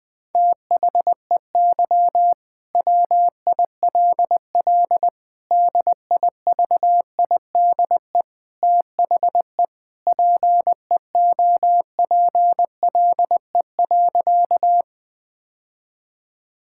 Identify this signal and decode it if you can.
Morse code, heard as a keyed tone, about 20 wpm, 700 Hz. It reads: THEY WILL DIVIDE THE PEOPLE.
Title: Sentences from Top 1600 Words 20wpm